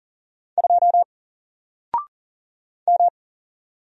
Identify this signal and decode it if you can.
Morse code, heard as a keyed tone, about 40 wpm, 700 Hz. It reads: 2 M
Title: Single Letter-Number 40wpm